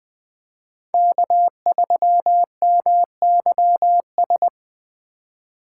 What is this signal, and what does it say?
Morse code, heard as a keyed tone, about 20 wpm, 700 Hz. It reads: K3MYS
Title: USA Call Signs - 5 Characters - 4x Character Spacing - 20wpm